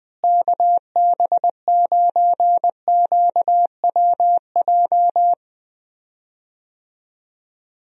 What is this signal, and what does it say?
Morse code, heard as a keyed tone, about 20 wpm, 700 Hz. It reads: KB9QWJ
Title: USA Call Signs - 6 Characters - Repeated 2x - 20wpm